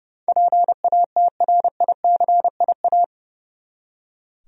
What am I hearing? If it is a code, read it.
Morse code, heard as a keyed tone, about 30 wpm, 700 Hz. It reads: PATRICIA
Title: USA Common Names - 30wpm